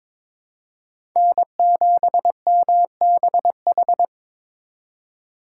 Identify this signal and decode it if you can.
Morse code, heard as a keyed tone, about 21 wpm, 700 Hz. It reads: N7MBH
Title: USA Call Signs - 5 Characters - 3x Character Spacing - 22wpm